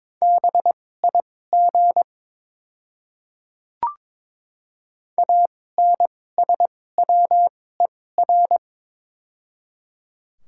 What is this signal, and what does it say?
Morse code, heard as a keyed tone, about 22 wpm, spaced as for 16 wpm, 700 Hz. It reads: BIG ANSWER
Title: 2x Character Spacing - Top 300 Words - 22wpm